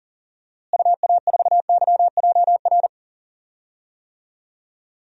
Morse code, heard as a keyed tone, about 40 wpm, 700 Hz. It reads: UA4YJR
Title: DXCC-100 Call Signs - 6 Characters - Repeated 3x - 40wpm